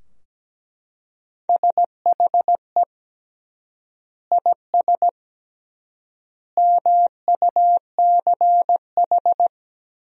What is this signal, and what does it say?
Morse code, heard as a keyed tone, about 17 wpm, 700 Hz. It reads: SHE IS MUCH
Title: 3x Word-Spacing - Sentences from Top 400 Words - 17wpm